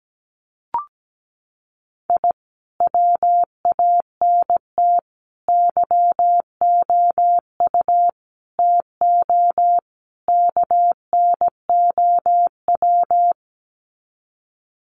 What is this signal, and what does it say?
Morse code, heard as a keyed tone, about 17 wpm, 700 Hz. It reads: I WANT YOU TO KNOW
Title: Sets of 5 Words Encore 17wpm